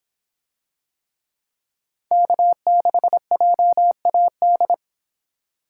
Morse code, heard as a keyed tone, about 25 wpm, 700 Hz. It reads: K6JAD